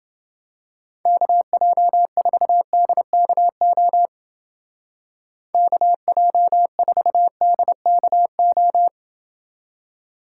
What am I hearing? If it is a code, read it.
Morse code, heard as a keyed tone, about 30 wpm, 700 Hz. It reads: KJ4DKO KJ4DKO